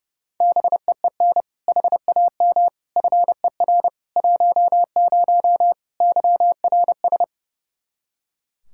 Morse code, heard as a keyed tone, about 30 wpm, 700 Hz. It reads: BEEN HAM FER 10 YRS
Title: Practice QSO - Exchange 3 - More Info - 30wpm